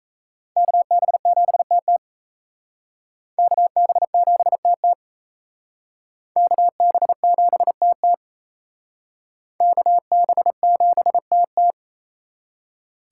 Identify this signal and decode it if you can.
Morse code, heard as a keyed tone, about 34 wpm, 700 Hz. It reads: KB7TT KB7TT KB7TT KB7TT
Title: Speed Racing - USA 2x2 Call Signs 28wpm